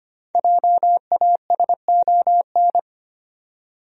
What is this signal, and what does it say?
Morse code, heard as a keyed tone, about 25 wpm, 700 Hz. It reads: JASON
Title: USA Common Names - 25wpm